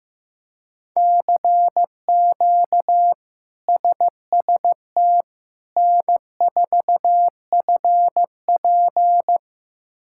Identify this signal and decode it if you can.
Morse code, heard as a keyed tone, about 15 wpm, 700 Hz. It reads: CQ SST N4FP
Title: K1USN SST QSO Practice - 15wpm